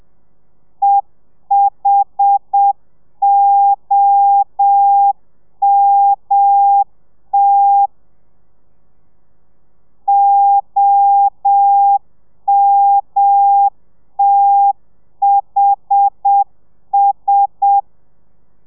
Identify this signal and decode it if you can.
Morse code, heard as a keyed tone, about 7 wpm, 800 Hz. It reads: EHOMT OMTHS
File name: telegrafia / i2tfj / Lez1 30